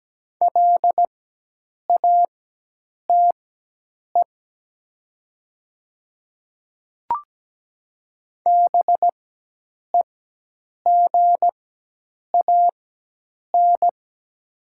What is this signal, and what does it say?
Morse code, heard as a keyed tone, about 17 wpm, spaced as for 8 wpm, 700 Hz. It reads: LATE BEGAN
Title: 4x Character Spacing - Top 300 Words - 17wpm